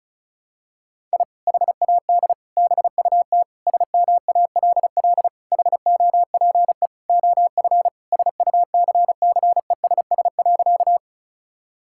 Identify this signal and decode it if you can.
Morse code, heard as a keyed tone, about 36 wpm, 700 Hz. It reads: I HAD BUT SMALL HOPE OF SUCCESS.
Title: Sentences from Top 1000 Words - Review Part 6 - 35wpm